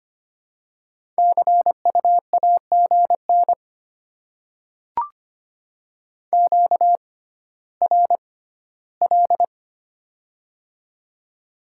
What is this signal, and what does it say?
Morse code, heard as a keyed tone, about 25 wpm, spaced as for 14 wpm, 700 Hz. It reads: CU AGN QRL
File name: Top 20 QSO Elements - 6x Character Spacing - 25wpm